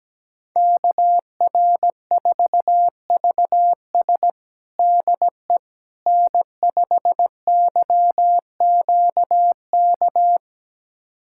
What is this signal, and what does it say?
Morse code, heard as a keyed tone, about 17 wpm, 700 Hz. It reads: KR4VS DE N5YQK
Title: USA Callsign DE Callsign - 5 Characters - 17wpm